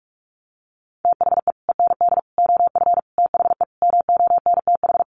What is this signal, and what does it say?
Morse code, heard as a keyed tone, about 44 wpm, 700 Hz. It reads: THE END OF THE MONTH
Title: Sets of 5 Words Encore 45wpm